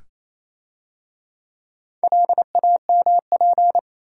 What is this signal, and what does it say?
Morse code, heard as a keyed tone, about 27 wpm, 700 Hz. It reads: LAMP
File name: Easy Words - Speed-Racing with Hints - 22wpm